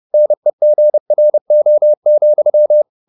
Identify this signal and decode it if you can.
Morse code, heard as a keyed tone, about 30 wpm, 600 Hz. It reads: NEGRO,